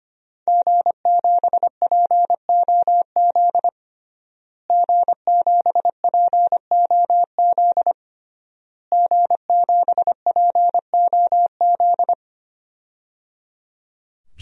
Morse code, heard as a keyed tone, about 25 wpm, 700 Hz. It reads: G7POZ G7POZ G7POZ
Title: DXCC-100 Call Signs - 5 Characters - Repeated 3x - 25wpm